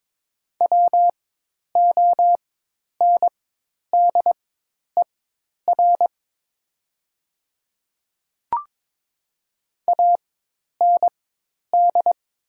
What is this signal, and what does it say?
Morse code, heard as a keyed tone, about 22 wpm, spaced as for 10 wpm, 700 Hz. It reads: WONDER AND
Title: 4x Character Spacing - Top 500 Words - 22wpm